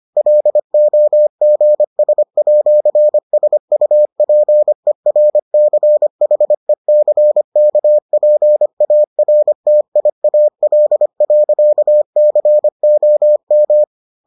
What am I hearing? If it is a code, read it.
Morse code, heard as a keyed tone, about 25 wpm, 600 Hz. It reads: LOGS@SUPERCHECKPARTIAL.COM